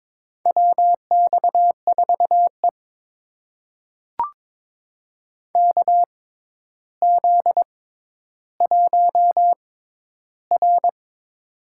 Morse code, heard as a keyed tone, about 22 wpm, spaced as for 11 wpm, 700 Hz. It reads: WX4E KZ1R
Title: USA Call Signs - 4 Characters - 6x Character Spacing - 22wpm